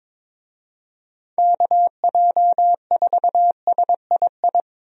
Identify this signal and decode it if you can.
Morse code, heard as a keyed tone, about 22 wpm, 700 Hz. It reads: KJ4SII
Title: USA Call Signs - 6 Characters - Repeated 2x - 22wpm